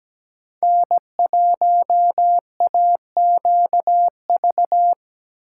Morse code, heard as a keyed tone, about 17 wpm, 700 Hz. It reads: N1AQV
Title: USA Call Signs - 5 Characters - 8x Character Spacing - 17wpm